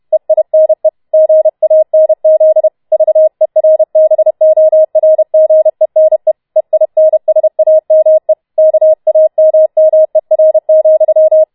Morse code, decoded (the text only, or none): EINE GANZ VERBORGENE EINSAME KAMMER,